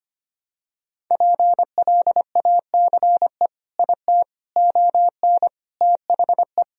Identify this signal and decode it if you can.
Morse code, heard as a keyed tone, about 25 wpm, 700 Hz. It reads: PLACE IT ON THE
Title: Sentences from Top 1000 Words - Review Part 2 - 25wpm